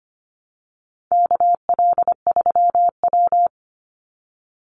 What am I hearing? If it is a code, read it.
Morse code, heard as a keyed tone, about 25 wpm, 700 Hz. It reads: KL3W
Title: Call Signs Repeated 3x - 25wpm